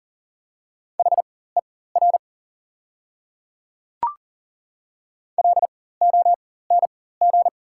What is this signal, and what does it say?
Morse code, heard as a keyed tone, about 40 wpm, spaced as for 19 wpm, 690 Hz. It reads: HER LONG
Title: 4x Character Spacing - Top 200 Words - 40wpm